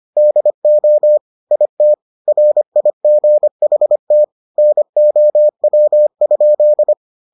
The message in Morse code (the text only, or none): DO IT RIGHT NOW?